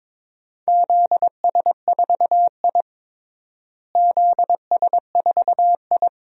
Morse code, heard as a keyed tone, about 22 wpm, 700 Hz. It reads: ZS4I ZS4I
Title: DXCC-100 Call Signs - 4 Characters - Repeated 2x - 22wpm